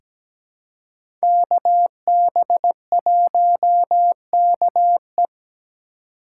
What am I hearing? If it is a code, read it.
Morse code, heard as a keyed tone, about 17 wpm, 700 Hz. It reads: KB1KE
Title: USA Call Signs - 5 Characters - 3x Character Spacing - 17wpm